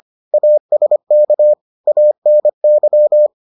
Morse code, heard as a keyed tone, about 25 wpm, 600 Hz. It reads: ASK ANY